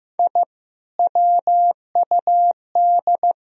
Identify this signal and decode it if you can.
Morse code, heard as a keyed tone, about 15 wpm, 700 Hz. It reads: I WUD
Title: Sentences from Top 200 Words and Abbreviations 15wpm